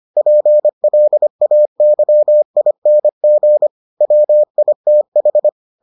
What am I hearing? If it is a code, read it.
Morse code, heard as a keyed tone, about 25 wpm, 600 Hz. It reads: PLAYING WITH